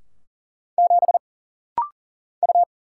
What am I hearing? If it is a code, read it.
Morse code, heard as a keyed tone, about 40 wpm, 700 Hz. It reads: 7 U
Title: Single Letter-Number - Rapid-Fire 40wpm